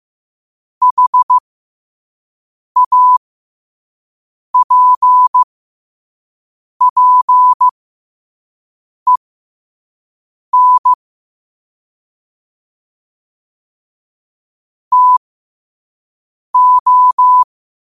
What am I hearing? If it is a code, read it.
Morse code, heard as a keyed tone, about 15 wpm, spaced as for 5 wpm, 1000 Hz. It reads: HAPPEN TO
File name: qotdfiles / 5wpm